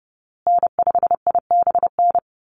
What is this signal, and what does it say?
Morse code, heard as a keyed tone, about 30 wpm, 700 Hz. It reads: N5IBN
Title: USA Call Signs Repeated 2x 30wpm